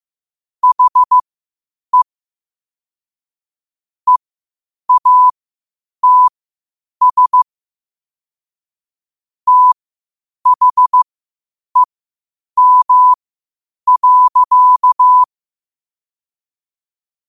Morse code, heard as a keyed tone, about 15 wpm, spaced as for 8 wpm, 1000 Hz. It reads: HE EATS THEM.
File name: qotdfiles / 8wpm